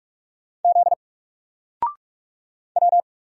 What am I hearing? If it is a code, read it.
Morse code, heard as a keyed tone, about 44 wpm, 700 Hz. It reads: Z W
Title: Single Letter-Number 45wpm